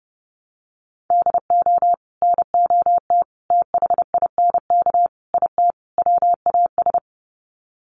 Morse code, heard as a keyed tone, about 30 wpm, 700 Hz. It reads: DO NOT THINK IT WAS
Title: Sets of 5 Words Encore 30wpm